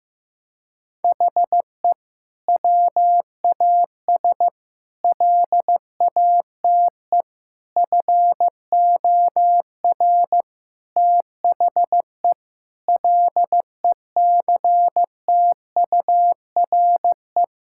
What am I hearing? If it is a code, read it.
Morse code, heard as a keyed tone, about 15 wpm, 700 Hz. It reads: HE WAS LATE FOR THE LECTURE
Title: Sentences from Top 1600 Words 15wpm